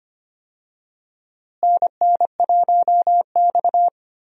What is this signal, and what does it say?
Morse code, heard as a keyed tone, about 25 wpm, 700 Hz. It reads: NN1X
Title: USA Call Signs - 4 Characters - Repeated 2x - 25wpm